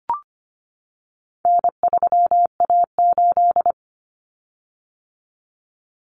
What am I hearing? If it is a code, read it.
Morse code, heard as a keyed tone, about 25 wpm, 700 Hz. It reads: N3A8
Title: Common Cut Numbers - 4-Digit 25wpm